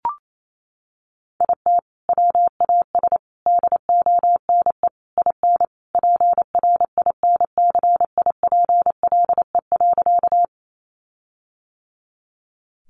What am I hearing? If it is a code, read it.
Morse code, heard as a keyed tone, about 28 wpm, 700 Hz. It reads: IT WAS DONE IN PRINCIPLE.
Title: Sentences from Top 1400 Words 28wpm